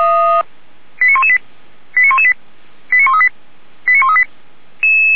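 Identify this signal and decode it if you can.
Five-tone selective call (ZVEI), heard as quick runs of five tones, tones of about 70 ms. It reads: 89118 89118 89138 89138
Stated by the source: Five-tone ZVEI selective call with code 89227, received by SDR and logged by feuerwehr-orlamuende fire brigade.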